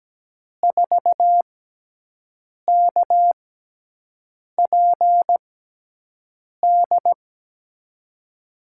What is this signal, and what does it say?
Morse code, heard as a keyed tone, about 17 wpm, spaced as for 6 wpm, 700 Hz. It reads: 4KPD
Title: USA Call Signs - 6 Characters - 6x Character Spacing - 17wpm